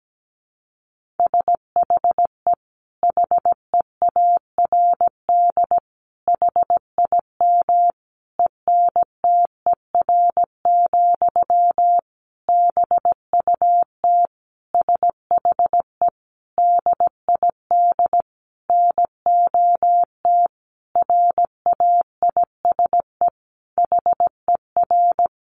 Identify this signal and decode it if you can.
Morse code, heard as a keyed tone, about 17 wpm, 700 Hz. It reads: SHE HEARD HIM ENTER, BUT SHE DID NOT RAISE HER
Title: Sentences from Top 1400 Words 17wpm